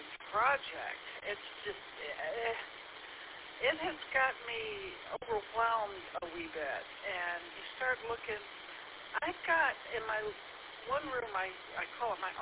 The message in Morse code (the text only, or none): none